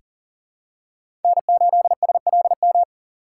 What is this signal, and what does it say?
Morse code, heard as a keyed tone, about 40 wpm, 700 Hz. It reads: N8SLM
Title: USA Call Signs - 5 Characters - 4x Character Spacing - 40wpm